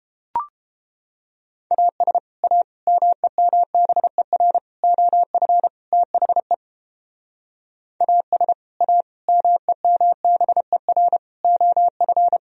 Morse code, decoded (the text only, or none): AS A MEMBER OF THE AS A MEMBER OF